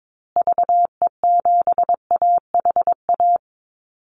Morse code, heard as a keyed tone, about 22 wpm, 700 Hz. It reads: VE7AHA
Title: Call Signs Repeated 2x - 22wpm